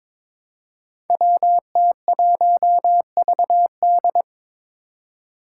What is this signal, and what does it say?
Morse code, heard as a keyed tone, about 23 wpm, 700 Hz. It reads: WT1VD
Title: Five Challenging Letter-Number Combinations 22wpm